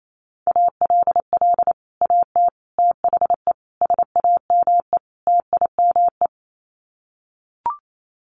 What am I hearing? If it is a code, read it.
Morse code, heard as a keyed tone, about 28 wpm, 700 Hz. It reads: ALL AT THE SAME TIME
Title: Sets of 5 Words Encore2 28wpm